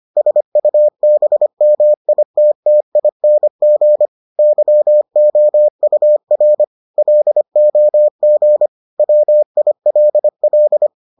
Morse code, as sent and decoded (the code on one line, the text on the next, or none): ... ..- -... -- .. - - .. -. --. / -.-- --- ..- .-. / .-.. --- --. / .-- .. .-.. .-..
SUBMITTING YOUR LOG WILL